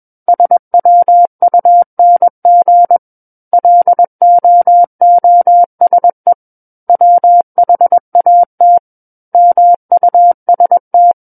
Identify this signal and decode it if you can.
Morse code, heard as a keyed tone, about 21 wpm, 700 Hz. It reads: SWUNG LOOSE WHAT MUST
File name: Book One / 12-Chapter 12 - 21 WPM